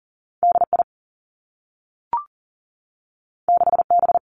Morse code, heard as a keyed tone, about 38 wpm, 700 Hz. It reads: DI 6B
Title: Two Challenging Letter-Number Combinations 40wpm